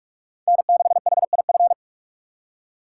Morse code, heard as a keyed tone, about 46 wpm, 690 Hz. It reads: N6HIF
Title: USA Call Signs - 5 Characters - 2x Character Spacing - 45wpm